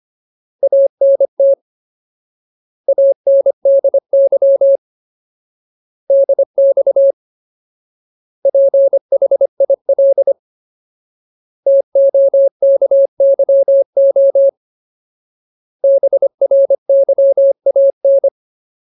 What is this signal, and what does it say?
Morse code, heard as a keyed tone, about 25 wpm, 550 Hz. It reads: ANT ANDY DX PHIL TOKYO BRYAN
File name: Ultimate Word List / 25WPM Ultimate